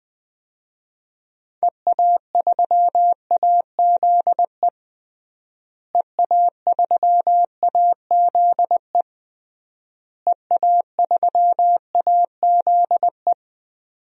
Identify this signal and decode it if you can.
Morse code, heard as a keyed tone, about 20 wpm, 700 Hz. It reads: EA3AZE EA3AZE EA3AZE
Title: DXCC-100 Call Signs - 6 Characters - Repeated 3x - 20wpm